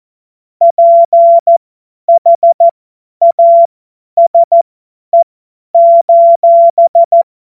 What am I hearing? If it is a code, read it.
Morse code, heard as a keyed tone, about 14 wpm, spaced as for 10 wpm, 680 Hz. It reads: PHASE:
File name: Morsekurs / Witze / Witz 09. Dezember 2025 50 F70